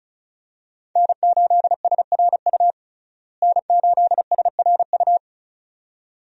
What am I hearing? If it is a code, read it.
Morse code, heard as a keyed tone, about 36 wpm, 700 Hz. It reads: N8SRU N8SRU
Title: USA Call Signs - 5 Characters - Repeated 2x - 35wpm